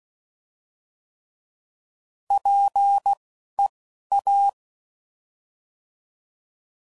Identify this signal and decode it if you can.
Morse code, heard as a keyed tone, about 16 wpm, spaced as for 12 wpm, 800 Hz. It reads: PEA
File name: Prefixes / By First Letter / Ps